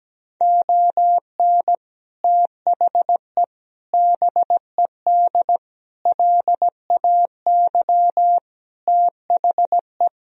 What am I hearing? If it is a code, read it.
Morse code, heard as a keyed tone, about 17 wpm, 700 Hz. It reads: ON THE BED LAY THE
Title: Sentences from Top 1000 Words - Review Part 2 - 17wpm